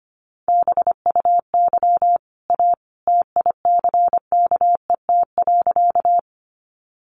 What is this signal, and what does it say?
Morse code, heard as a keyed tone, about 25 wpm, 700 Hz. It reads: BUY A TICKET.
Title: Sentences from Top 1400 Words 25wpm